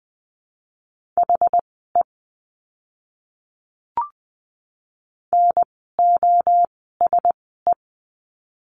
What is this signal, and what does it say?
Morse code, heard as a keyed tone, about 20 wpm, spaced as for 15 wpm, 700 Hz. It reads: HE NOSE
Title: 061 Lesson 11f - Letter h - Words 20wpm